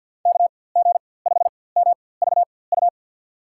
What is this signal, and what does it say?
Morse code, heard as a keyed tone, about 52 wpm, spaced as for 24 wpm, 700 Hz. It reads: KG5MVU
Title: USA Call Signs - 4x Character Spacing - 50wpm